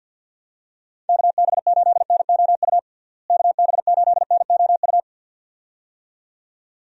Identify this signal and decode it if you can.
Morse code, heard as a keyed tone, about 52 wpm, 700 Hz. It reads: KB8NOU KB8NOU
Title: USA Call Signs - 6 Characters - Repeated 2x - 50wpm